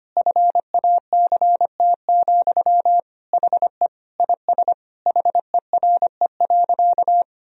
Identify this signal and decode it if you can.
Morse code, heard as a keyed tone, about 25 wpm, 700 Hz. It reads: FACT, HE IS HERE.
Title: Sentences from Top 1000 Words - Review Part 3 - 25wpm